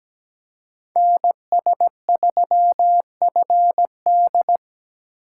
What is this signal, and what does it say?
Morse code, heard as a keyed tone, about 20 wpm, 700 Hz. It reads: NS3FD
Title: Speed Racing - USA 2x2 Call Signs 17wpm